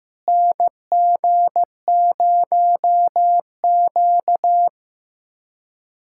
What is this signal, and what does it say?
Morse code, heard as a keyed tone, about 16 wpm, 700 Hz. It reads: NG0Q
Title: USA Callsign DE Callsign - 4 Characters - 15wpm